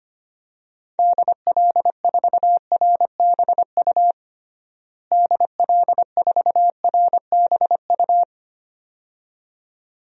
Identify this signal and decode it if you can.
Morse code, heard as a keyed tone, about 25 wpm, 700 Hz. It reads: DL4RBU DL4RBU
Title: DXCC-100 Call Signs - 6 Characters - Repeated 2x - 25wpm